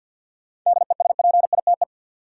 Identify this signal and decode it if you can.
Morse code, heard as a keyed tone, about 52 wpm, 700 Hz. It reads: DESPITE
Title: Sets of 4 Words - New Word Drill - 50wpm